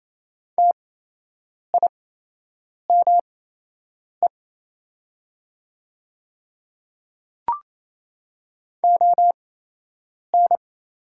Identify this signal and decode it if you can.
Morse code, heard as a keyed tone, about 28 wpm, spaced as for 8 wpm, 700 Hz. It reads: TIME ON
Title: 8x Character Spacing - Top 100 Words - 28wpm